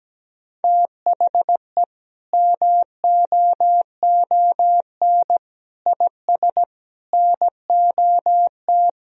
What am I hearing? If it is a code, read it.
Morse code, heard as a keyed tone, about 17 wpm, 700 Hz. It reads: THE MOON IS NOT
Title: Sentences from Top 1000 Words - Review Part 3 - 17wpm